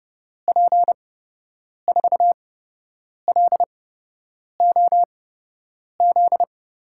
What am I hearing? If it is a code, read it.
Morse code, heard as a keyed tone, about 32 wpm, spaced as for 8 wpm, 700 Hz. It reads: P4LOZ
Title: USA Call Signs - 6 Characters - 8x Character Spacing - 30wpm